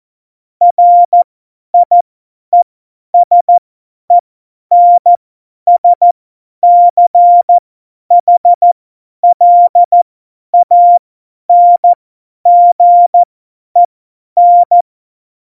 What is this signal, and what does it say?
Morse code, heard as a keyed tone, about 14 wpm, spaced as for 10 wpm, 700 Hz. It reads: RIESENSCHLANGEN
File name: Morsekurs / Witze / Witz 10 April 2018, 50 F70